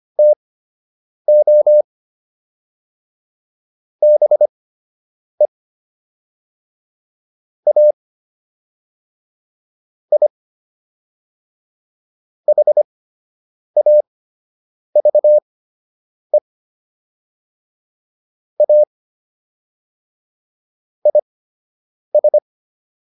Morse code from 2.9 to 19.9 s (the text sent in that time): BE A I HAVE A